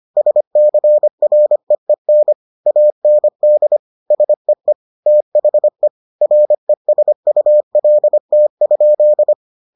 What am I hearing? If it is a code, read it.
Morse code, heard as a keyed tone, about 25 wpm, 600 Hz. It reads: SCREEN AND SEE THE RESULT?